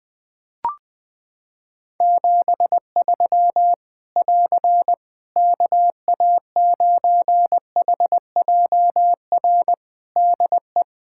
Morse code, heard as a keyed tone, about 20 wpm, 700 Hz. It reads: 73 <AR> KA9HJR DE
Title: Practice QSO - Exchange 4 - Ending - 20wpm